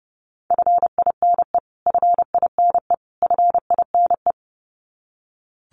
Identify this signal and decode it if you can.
Morse code, heard as a keyed tone, about 30 wpm, 700 Hz. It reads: FINE FINE FINE
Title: Top 500 Words Repeated 30wpm